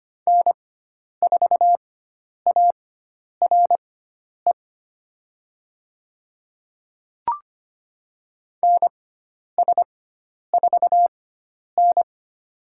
Morse code, heard as a keyed tone, about 25 wpm, spaced as for 10 wpm, 700 Hz. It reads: N4ARE NS4N
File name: Lesson 16g - Letter c - Call Signs 25wpm